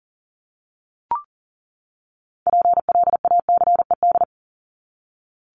Morse code, heard as a keyed tone, about 40 wpm, 700 Hz. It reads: PLACED
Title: ED Suffix - Top 100 Words with Base Words 40wpm